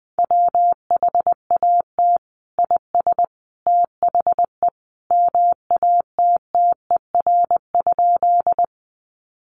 Morse code, heard as a keyed tone, about 20 wpm, 700 Hz. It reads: WHAT IS THE MATTER?
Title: Sentences from Top 700 Words 20wpm